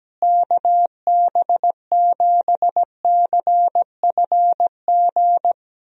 Morse code, heard as a keyed tone, about 17 wpm, 700 Hz. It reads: KB7CFG
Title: USA Call Signs - 6 Characters - Repeated 3x - 17wpm